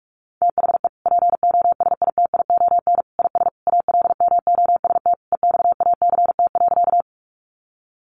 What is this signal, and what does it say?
Morse code, heard as a keyed tone, about 44 wpm, 700 Hz. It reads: THE POSITION IS ALMOST EXACT.
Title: Sentences from Top 1300 Words 45wpm